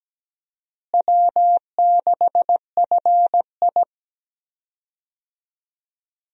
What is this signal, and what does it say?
Morse code, heard as a keyed tone, about 17 wpm, 700 Hz. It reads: W6FI
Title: USA Call Signs - 4 Characters - Repeated 3x - 17wpm